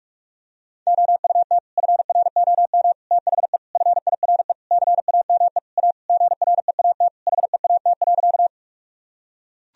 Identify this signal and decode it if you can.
Morse code, heard as a keyed tone, about 41 wpm, 700 Hz. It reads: OUT FROM THE FIRE CAME A GREAT HEAT.
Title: Sentences from Top 1000 Words - Review Part 3 - 45wpm